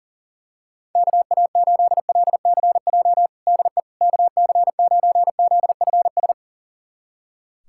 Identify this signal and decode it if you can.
Morse code, heard as a keyed tone, about 40 wpm, 700 Hz. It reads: KA8LCJ DE KC9ZFS
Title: USA Callsign DE Callsign - 6 Characters - 40wpm